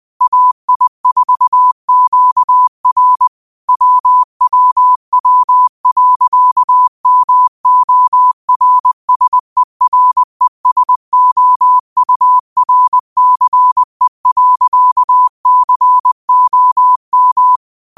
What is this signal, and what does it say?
Morse code, heard as a keyed tone, about 20 wpm, 1000 Hz. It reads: AI4QR WWW.MORSERESOURCE.COM
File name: qotdfiles / 20wpm